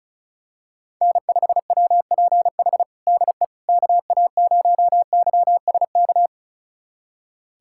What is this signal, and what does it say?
Morse code, heard as a keyed tone, about 36 wpm, 700 Hz. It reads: N5WPH DE KA0YSK
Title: USA Callsign DE Callsign 35wpm